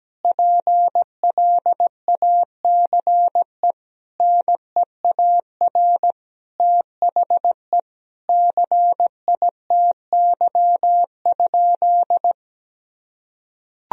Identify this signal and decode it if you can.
Morse code, heard as a keyed tone, about 17 wpm, 700 Hz. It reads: PLACE NEAR THE CITY?
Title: Sentences from Top 1000 Words - Review Part 3 - 17wpm